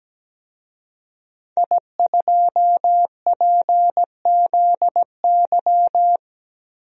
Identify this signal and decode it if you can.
Morse code, heard as a keyed tone, about 17 wpm, 700 Hz. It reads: I2PZY
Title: DXCC-100 Call Signs - 5 Characters - 17wpm